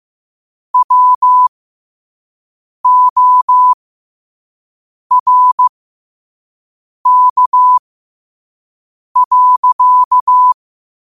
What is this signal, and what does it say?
Morse code, heard as a keyed tone, about 15 wpm, spaced as for 5 wpm, 1000 Hz. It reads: WORK.
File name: qotdfiles / 5wpm